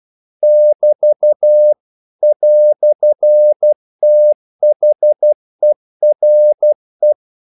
Morse code, heard as a keyed tone, about 12 wpm, 600 Hz. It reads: <BT> "THERE
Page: English quotations in CW (telegraphy - morse)